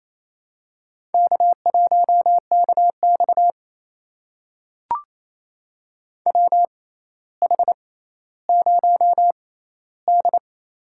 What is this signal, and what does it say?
Morse code, heard as a keyed tone, about 28 wpm, spaced as for 14 wpm, 700 Hz. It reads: K1KX WH0D
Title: USA Call Signs - 4 Characters - 6x Character Spacing - 28wpm